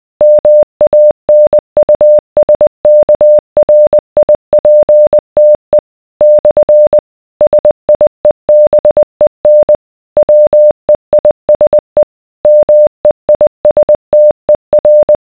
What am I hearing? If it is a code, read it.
Morse code, heard as a keyed tone, about 20 wpm, 600 Hz. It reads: MANUSKRIPTE / SIEBEN WEISE MEISTER